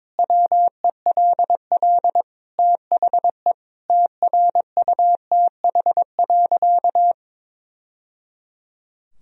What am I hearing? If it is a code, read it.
Morse code, heard as a keyed tone, about 22 wpm, 700 Hz. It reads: WELL THE TRUTH.
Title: Sentences from Top 1000 Words - Review Part 3 - 22wpm